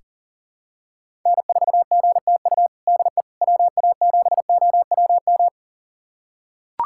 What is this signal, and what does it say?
Morse code, heard as a keyed tone, about 37 wpm, 700 Hz. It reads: N4GTU DE WA7OWM
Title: USA Callsign DE Callsign 40wpm